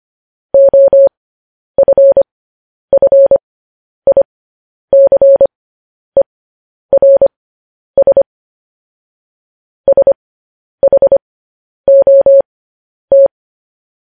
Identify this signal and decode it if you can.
Morse code, heard as a keyed tone, about 25 wpm, spaced as for 10 wpm, 550 Hz. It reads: OFFICERS SHOT
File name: podcast / News headlines-1x-25x10